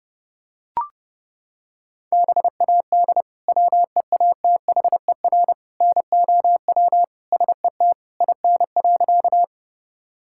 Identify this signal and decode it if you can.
Morse code, heard as a keyed tone, about 30 wpm, 700 Hz. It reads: BAD WEATHER NOW SET IN.
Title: Sentences from Top 1000 Words - Review Part 5 - 30wpm